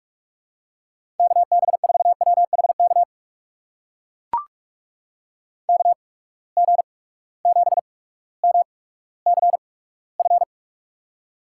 Morse code, heard as a keyed tone, about 44 wpm, spaced as for 19 wpm, 700 Hz. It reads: KB4WHK KG7MCF
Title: USA Call Signs - 6 Characters - 8x Character Spacing - 45wpm